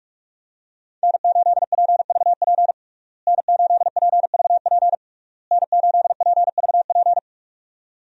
Morse code, heard as a keyed tone, about 41 wpm, 700 Hz. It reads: N8PVP N8PVP N8PVP
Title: USA Call Signs - 5 Characters - Repeated 3x - 45wpm